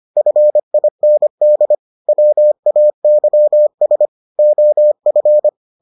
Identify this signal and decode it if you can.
Morse code, heard as a keyed tone, about 25 wpm, 600 Hz. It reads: FIND WAYS OF